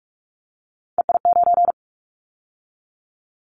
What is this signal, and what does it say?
Morse code, heard as a keyed tone, about 46 wpm, 700 Hz. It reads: EI9
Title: Callsign Prefixes for Top 50 DXCC - 45wpm